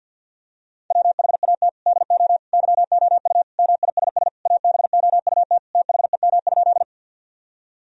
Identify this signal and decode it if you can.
Morse code, heard as a keyed tone, about 50 wpm, 700 Hz. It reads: WHAT DO YOU MISS ABOUT THEM?
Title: Sentences from Top 1000 Words - Review Part 3 - 50wpm